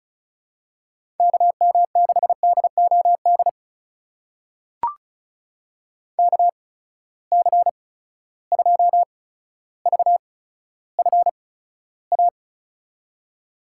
Morse code, heard as a keyed tone, about 36 wpm, spaced as for 15 wpm, 700 Hz. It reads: KM6DOD KC2VFA
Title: USA Call Signs - 6 Characters - 8x Character Spacing - 35wpm